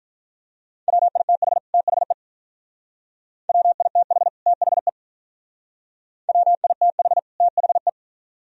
Morse code, heard as a keyed tone, about 42 wpm, 700 Hz. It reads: WITH THE WITH THE WITH THE
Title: Speed-Racing - Sets of 2 Words 35wpm